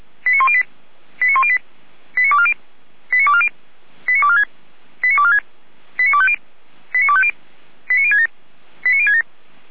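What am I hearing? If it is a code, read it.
Five-tone selective call (ZVEI), heard as quick runs of five tones, tones of about 70 ms. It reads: 89108 89108 89240 89240 89246 89246 89250 89250 89067 89067
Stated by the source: ZVEI five-tone call received by SDR, from the Feuerwehr Orlamünde audio archive.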